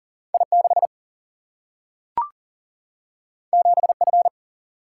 Morse code, heard as a keyed tone, about 39 wpm, 700 Hz. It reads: I 6 7 F
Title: Single-Letter Single-Number 40wpm